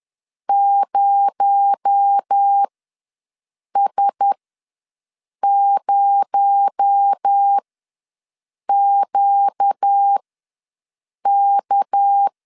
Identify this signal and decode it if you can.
Morse code, heard as a keyed tone, about 10 wpm, spaced as for 6 wpm, 800 Hz. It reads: 0S0QK
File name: LEZIONI / 6 PKX0 / 6 PKX0 ws 10wpm 4220